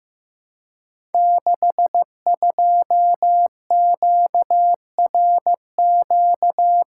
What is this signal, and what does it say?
Morse code, heard as a keyed tone, about 15 wpm, 700 Hz. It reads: 62QRQ
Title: Code Groups - 5 Letter:Numbers - 15wpm